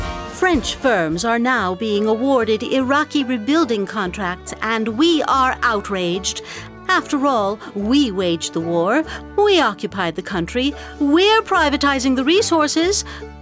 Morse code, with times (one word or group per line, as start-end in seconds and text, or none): none